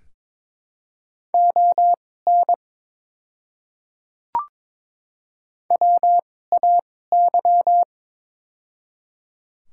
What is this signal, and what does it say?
Morse code, heard as a keyed tone, about 22 wpm, spaced as for 16 wpm, 700 Hz. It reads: ON WAY
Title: 2x Character Spacing - Top 100 Words - 22wpm